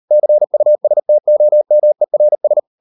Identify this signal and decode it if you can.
Morse code, heard as a keyed tone, about 38 wpm, 600 Hz. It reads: CUSTOMERS